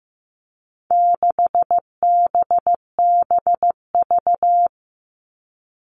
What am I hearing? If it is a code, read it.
Morse code, heard as a keyed tone, about 15 wpm, 700 Hz. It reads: 6BBV
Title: All Cut Numbers - 4-Digit 15wpm